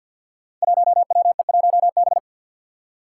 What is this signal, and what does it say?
Morse code, heard as a keyed tone, about 48 wpm, 700 Hz. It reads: 1WE1B
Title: Five Challenging Letter-Number Combinations 50wpm